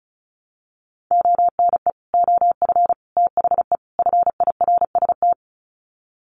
Morse code, heard as a keyed tone, about 34 wpm, 700 Hz. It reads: ONE OF THE FIRST
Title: Sets of 4 Words 35wpm